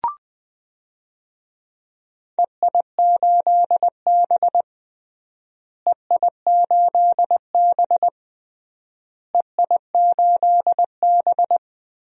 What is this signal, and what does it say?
Morse code, heard as a keyed tone, about 20 wpm, 700 Hz. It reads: EI8B EI8B EI8B
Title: DXCC-100 Call Signs - 4 Characters - Repeated 3x - 20wpm